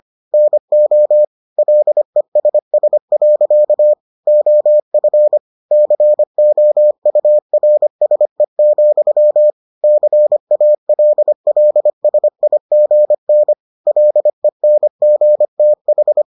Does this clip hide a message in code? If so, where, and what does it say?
Yes: Morse code from 0.3 to 16.2 s, heard as a keyed tone, about 25 wpm, 600 Hz. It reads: NO LESS. OF COURSE, CALLSIGN LENGTH